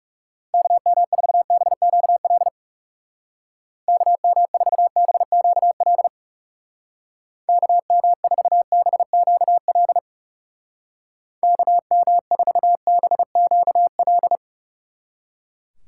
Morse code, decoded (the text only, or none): KM4BQL KM4BQL KM4BQL KM4BQL